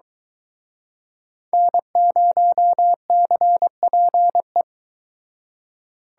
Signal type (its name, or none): Morse code